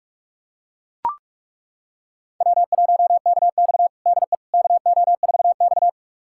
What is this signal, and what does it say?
Morse code, heard as a keyed tone, about 41 wpm, 700 Hz. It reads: W1KX DE KO4X
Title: USA Callsign DE Callsign - 4 Characters - 45wpm